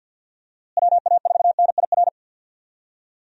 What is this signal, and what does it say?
Morse code, heard as a keyed tone, about 52 wpm, 700 Hz. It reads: WA4NIR
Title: USA Call Signs - 6 Characters - Repeated 3x - 50wpm